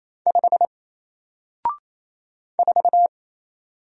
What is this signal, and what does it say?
Morse code, heard as a keyed tone, about 27 wpm, 700 Hz. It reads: 5 4
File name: Single-Letter-Number-28wpm